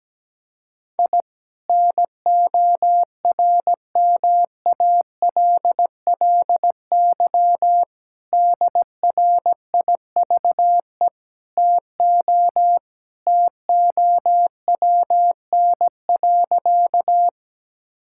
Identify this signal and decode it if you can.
Morse code, heard as a keyed tone, about 17 wpm, 700 Hz. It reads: I NORMALLY DRIVE TO TOWN.
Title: Sentences from Top 1000 Words - Review Part 6 - 17wpm